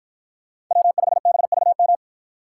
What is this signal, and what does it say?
Morse code, heard as a keyed tone, about 53 wpm, 700 Hz. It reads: W5BVM